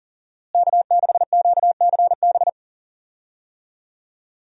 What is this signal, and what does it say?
Morse code, heard as a keyed tone, about 37 wpm, 700 Hz. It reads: K6QCB
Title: USA Call Signs - 5 Characters - Repeated 2x - 40wpm